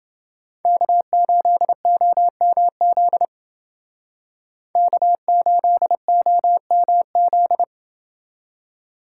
Morse code, heard as a keyed tone, about 29 wpm, 700 Hz. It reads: K8OMZ K8OMZ